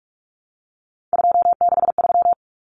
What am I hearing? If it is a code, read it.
Morse code, heard as a keyed tone, about 44 wpm, 700 Hz. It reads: 263
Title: Numbers - 3-Digit 45wpm